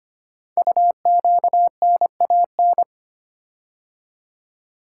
Morse code, heard as a keyed tone, about 25 wpm, 700 Hz. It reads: UQNAN
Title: Code Groups - 5 Letters - 25wpm